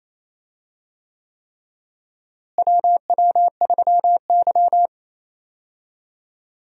Morse code, heard as a keyed tone, about 27 wpm, 700 Hz. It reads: WW3Y
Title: USA Call Signs - 4 Characters - Repeated 3x - 28wpm